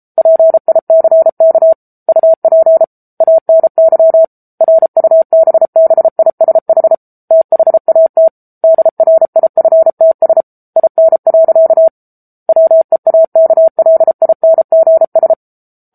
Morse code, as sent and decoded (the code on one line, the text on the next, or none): .--. .. -.-. -.- / ..- .--. / .- -. -.-- / .-. ..- -... -... .. ... .... / - .... .- - / -.. .-. .. ..-. - ... / .. -. .-.-.- / .-- . .- -.- .-.. .. -. --. ...
PICK UP ANY RUBBISH THAT DRIFTS IN. WEAKLINGS